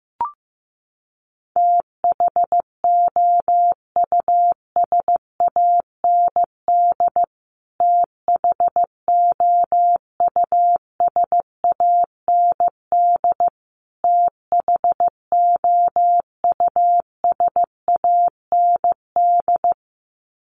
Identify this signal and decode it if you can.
Morse code, heard as a keyed tone, about 15 wpm, 700 Hz. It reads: THOUSAND THOUSAND THOUSAND
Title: Top 500 Words Repeated 15wpm